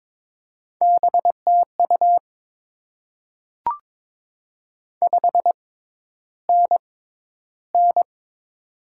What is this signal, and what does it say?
Morse code, heard as a keyed tone, about 22 wpm, spaced as for 11 wpm, 690 Hz. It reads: BTU 5NN
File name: Top 20 QSO Elements - 6x Character Spacing - 22wpm